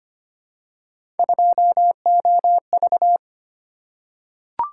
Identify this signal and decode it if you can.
Morse code, heard as a keyed tone, about 25 wpm, 700 Hz. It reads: 2OV
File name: Three Letter-Number Combinations - 25wpm